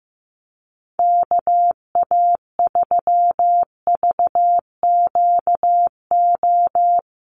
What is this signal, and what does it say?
Morse code, heard as a keyed tone, about 15 wpm, 700 Hz. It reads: KA3VQO
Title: USA Call Signs Repeated 2x 15wpm